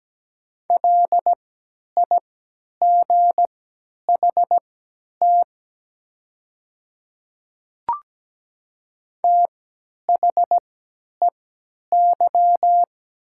Morse code, heard as a keyed tone, about 17 wpm, spaced as for 10 wpm, 700 Hz. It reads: LIGHT THEY
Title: 3x Character Spacing - Top 200 Words - 17wpm